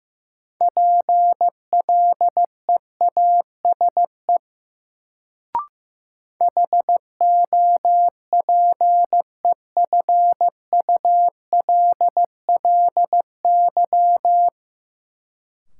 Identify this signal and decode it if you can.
Morse code, heard as a keyed tone, about 15 wpm, 700 Hz. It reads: PLEASE HOPEFULLY
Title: Sentences from Top 1500 Words - New Word Drill Alt - 15wpm